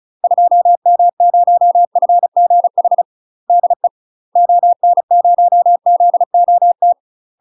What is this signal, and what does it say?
Morse code, heard as a keyed tone, about 34 wpm, 700 Hz. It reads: 2M0FGH DE ON0ZOT
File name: UK MASTER QSOs TRAINING / 35WPM / UK MASTER QSOs 0039